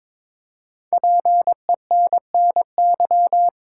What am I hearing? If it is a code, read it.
Morse code, heard as a keyed tone, about 22 wpm, 700 Hz. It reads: PENNY
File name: 5 Letter Words - First 3 Letters Given - 22wpm